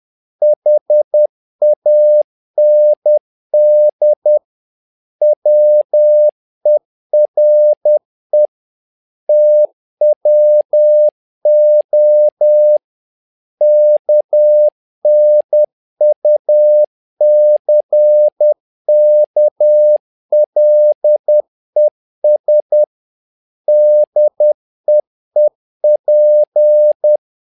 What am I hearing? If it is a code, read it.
Morse code, heard as a keyed tone, about 10 wpm, 600 Hz. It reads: HAND WERE TWO KNUCKLES DEEP
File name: learning center / tower trouble / tower trouble 10wpm